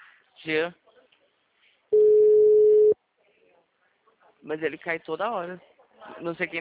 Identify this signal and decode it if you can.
Ringback tone (Europe, 425 Hz), with speech.